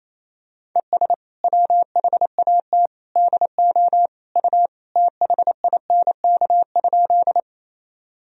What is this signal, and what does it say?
Morse code, heard as a keyed tone, about 28 wpm, 700 Hz. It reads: ES WHAT DO U THINK?
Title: Sentences from Top 200 Words and Abbreviations 28wpm